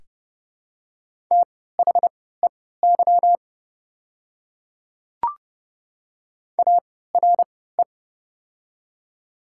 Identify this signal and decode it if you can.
Morse code, heard as a keyed tone, about 32 wpm, spaced as for 17 wpm, 700 Hz. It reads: THEY ARE
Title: 3x Character Spacing - Top 100 Words - 30wpm